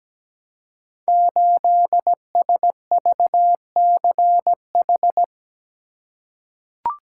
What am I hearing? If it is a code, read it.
Morse code, heard as a keyed tone, about 17 wpm, 700 Hz. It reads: 8SVCH
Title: Code Groups - 5 Letter:Numbers - 17wpm